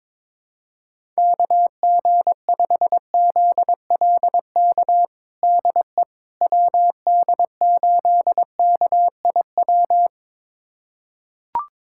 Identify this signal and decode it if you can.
Morse code, heard as a keyed tone, about 22 wpm, 700 Hz. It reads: KG5ZLK DE WD8KIW
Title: USA Callsign DE Callsign - 6 Characters - 22wpm